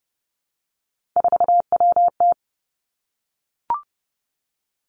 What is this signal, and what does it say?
Morse code, heard as a keyed tone, about 30 wpm, 700 Hz. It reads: 4WT
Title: Three Challenging Letter-Number Combinations 30wpm